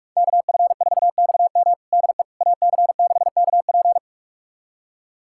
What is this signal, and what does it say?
Morse code, heard as a keyed tone, about 42 wpm, 700 Hz. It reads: KF4XM DE AC6KP
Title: USA Callsign DE Callsign - 5 Characters - 45wpm